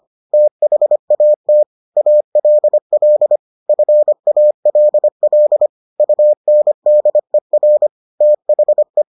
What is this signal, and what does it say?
Morse code, heard as a keyed tone, about 25 wpm, 600 Hz. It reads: THAT ALL FALL UNDER THE